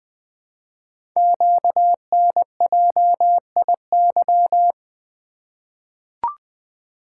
Morse code, heard as a keyed tone, about 20 wpm, 700 Hz. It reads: QNJIY
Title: Code Groups - 5 Letters - 20wpm